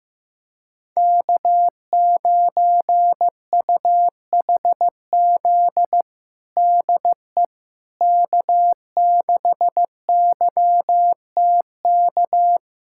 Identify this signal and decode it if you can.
Morse code, heard as a keyed tone, about 15 wpm, 700 Hz. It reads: K9UHZ DE K6YTK
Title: USA Callsign DE Callsign - 5 Characters - 15wpm